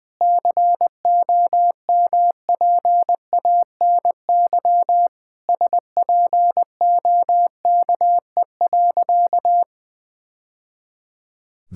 Morse code, heard as a keyed tone, about 20 wpm, 700 Hz. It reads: COMPANY SPOKE.
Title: Sentences with 3-Word Phrases - 20wpm